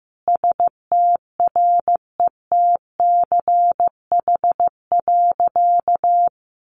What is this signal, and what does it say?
Morse code, heard as a keyed tone, about 15 wpm, 700 Hz. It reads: STRETCH.
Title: Easy Phrases 15wpm